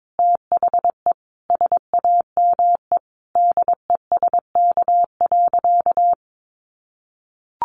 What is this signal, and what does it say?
Morse code, heard as a keyed tone, about 22 wpm, 700 Hz. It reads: THE SAME DESK.
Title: A Sound Of Thunder 22wpm